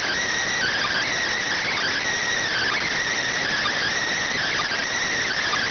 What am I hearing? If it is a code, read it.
Emergency Alert System (SAME) tones (header bursts), heard as harsh buzzing bursts. It is heard from 0.5 to 4.8 s.